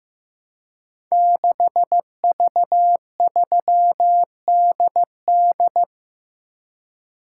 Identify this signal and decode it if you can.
Morse code, heard as a keyed tone, about 15 wpm, 700 Hz. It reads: 6V3DD
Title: Code Groups - 5 Letter:Numbers - 15wpm